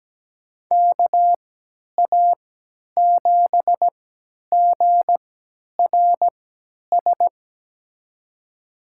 Morse code, heard as a keyed tone, about 17 wpm, spaced as for 10 wpm, 700 Hz. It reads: KA7GRS
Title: USA Call Signs - 6 Characters - 3x Character Spacing - 17wpm